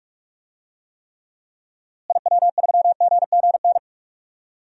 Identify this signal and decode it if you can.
Morse code, heard as a keyed tone, about 46 wpm, 700 Hz. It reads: IW3GGN
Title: DXCC-100 Call Signs - 6 Characters - Repeated 2x - 45wpm